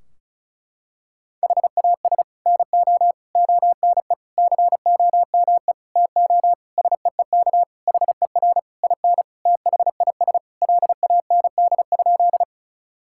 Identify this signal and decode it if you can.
Morse code, heard as a keyed tone, about 34 wpm, 700 Hz. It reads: HAS NO ONE COME TO SEEK HER IN THIS LAND?
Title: Sentences from Top 1600 Words 35wpm